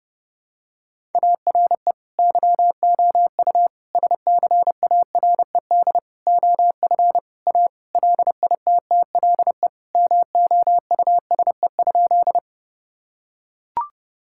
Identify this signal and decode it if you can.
Morse code, heard as a keyed tone, about 30 wpm, 700 Hz. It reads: ARE YOU SCARED OF A LITTLE MOUSE?
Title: Sentences from Top 1000 Words - Review Part 5 - 30wpm